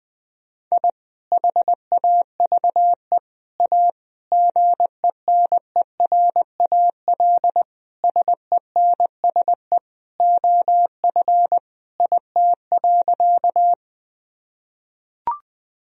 Morse code, heard as a keyed tone, about 20 wpm, 700 Hz. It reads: I HAVE A GENERAL SENSE OF IT.
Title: Sentences from Top 1000 Words - Review Part 4 - 20wpm